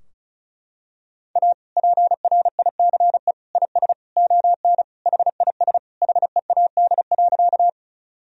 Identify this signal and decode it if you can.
Morse code, heard as a keyed tone, about 34 wpm, 700 Hz. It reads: A PRICE IS ON HIS HEAD.
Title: Sentences from Top 1000 Words - Review Part 4 - 35wpm